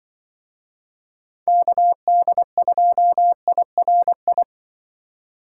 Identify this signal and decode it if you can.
Morse code, heard as a keyed tone, about 25 wpm, 700 Hz. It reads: KD2IRI
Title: Speed Racing - USA 2x3 Call Signs 20wpm